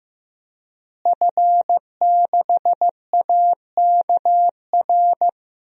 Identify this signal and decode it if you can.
Morse code, heard as a keyed tone, about 15 wpm, 700 Hz. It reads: F6AKR
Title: DXCC-100 Call Signs - 5 Characters - Repeated 2x - 15wpm